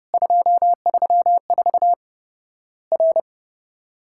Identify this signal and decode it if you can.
Morse code, heard as a keyed tone, about 30 wpm, 700 Hz. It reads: 234 R
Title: Worked All Europe DX Contest - QTC Exchange - 30wpm